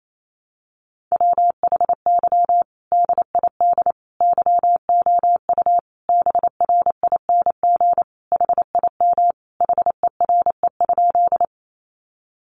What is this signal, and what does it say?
Morse code, heard as a keyed tone, about 28 wpm, 700 Hz. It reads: WHY DID YOU BRING HIM HERE?
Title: Sentences from Top 500 Words 28wpm